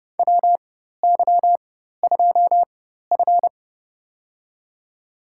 Morse code, heard as a keyed tone, about 30 wpm, spaced as for 14 wpm, 700 Hz. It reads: WY2F
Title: USA Call Signs - 4 Characters - 4x Character Spacing - 30wpm